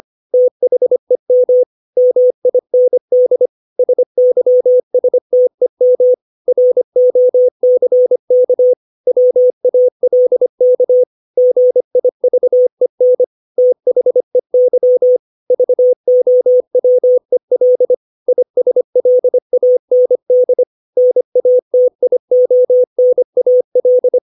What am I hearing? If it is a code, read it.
Morse code, heard as a keyed tone, about 25 wpm, 500 Hz. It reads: THEM MIND SYSTEM ROCK WALK GIVEN THEY VOWEL ISLAND NATIONAL
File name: 632 words / 632 Words 25WPM